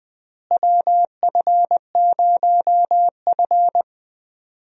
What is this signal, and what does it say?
Morse code, heard as a keyed tone, about 20 wpm, 700 Hz. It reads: WF0F
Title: USA Call Signs - 4 Characters - Repeated 2x - 20wpm